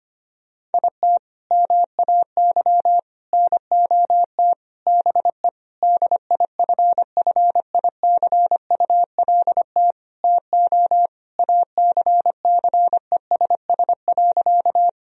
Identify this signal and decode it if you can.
Morse code, heard as a keyed tone, about 25 wpm, 700 Hz. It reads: IT MAY NOT BE DIFFICULT TO ACCESS.
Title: Sentences from Top 1000 Words - Review Part 4 - 25wpm